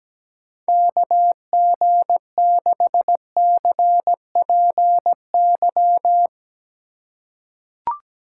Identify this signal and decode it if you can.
Morse code, heard as a keyed tone, about 17 wpm, 700 Hz. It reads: KG6CPY
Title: USA Call Signs - 6 Characters - 4x Character Spacing - 17wpm